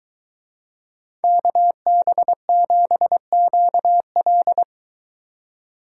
Morse code, heard as a keyed tone, about 21 wpm, 700 Hz. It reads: KB7QL